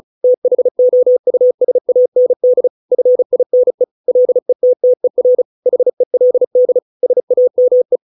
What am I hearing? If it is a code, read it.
Morse code, heard as a keyed tone, about 34 wpm, 500 Hz. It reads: THOUSAND FINE LETTER HELD SAME